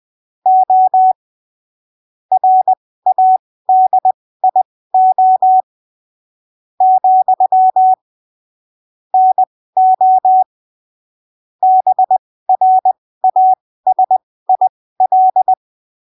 Morse code, heard as a keyed tone, about 20 wpm, spaced as for 13 wpm, 750 Hz. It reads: O RADIO, NO BRASIL
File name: arquivos / texto / texto 14 20ppm